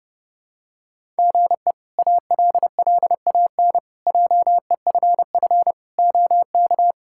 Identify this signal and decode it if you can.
Morse code, heard as a keyed tone, about 30 wpm, 700 Hz. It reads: GE ALLAN JEFF OK
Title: K1USN SST QSO Practice - 30wpm